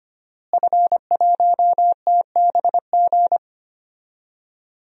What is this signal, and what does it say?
Morse code, heard as a keyed tone, about 25 wpm, 700 Hz. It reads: F1TBG
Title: DXCC-100 Call Signs - 5 Characters - Repeated 3x - 25wpm